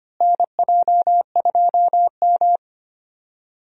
Morse code, heard as a keyed tone, about 25 wpm, 700 Hz. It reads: NJ2M
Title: USA Call Signs - 4 Characters - Repeated 2x - 25wpm